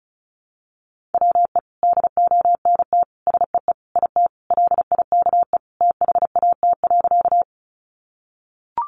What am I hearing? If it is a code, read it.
Morse code, heard as a keyed tone, about 34 wpm, 700 Hz. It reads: WE DONT SEE IT LIKE THAT.
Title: Sentences from Top 300 Words 35wpm